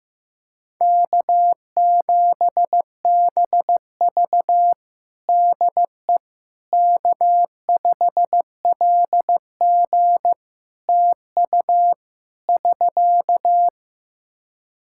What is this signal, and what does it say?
Morse code, heard as a keyed tone, about 15 wpm, 700 Hz. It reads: K7BV DE K5LG TU <SK>
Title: K1USN SST QSO Practice 15wpm